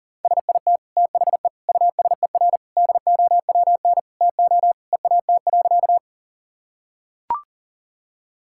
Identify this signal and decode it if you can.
Morse code, heard as a keyed tone, about 40 wpm, 700 Hz. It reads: SIT THE USER DOWN TO EAT.
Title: Sentences from Top 1000 Words - Review Part 5 - 40wpm